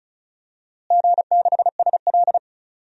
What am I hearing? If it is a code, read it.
Morse code, heard as a keyed tone, about 35 wpm, 700 Hz. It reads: G6SL
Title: DXCC-100 Call Signs - 4 Characters - Repeated 2x - 35wpm